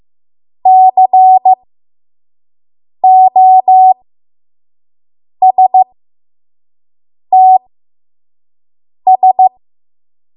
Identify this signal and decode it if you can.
Morse code, heard as a keyed tone, about 15 wpm, spaced as for 5 wpm, 750 Hz. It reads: COSTS
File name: Morse / 05WPM